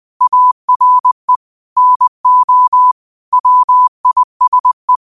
Morse code, heard as a keyed tone, about 20 wpm, 1000 Hz. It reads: ARE NO WISE
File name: qotdfiles / 20wpm